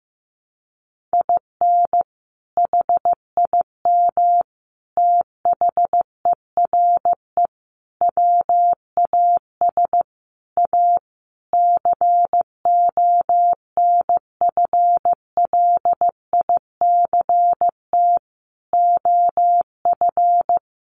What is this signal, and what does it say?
Morse code, heard as a keyed tone, about 15 wpm, 700 Hz. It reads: IN HIM THERE WAS A CONFLICT OF
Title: Sentences from Top 1300 Words 15wpm